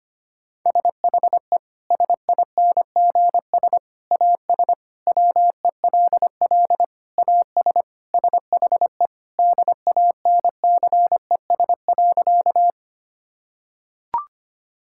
Morse code, heard as a keyed tone, about 25 wpm, 700 Hz. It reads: SHE SINGS AS WELL AS SHE DANCES.
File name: practice / 25wpm / Sentences with 3-Word Phrases - 25wpm